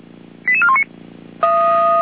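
Five-tone selective call (ZVEI), heard as quick runs of five tones, tones of about 70 ms. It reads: 88419
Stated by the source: Five-tone ZVEI selective call received on an SDR.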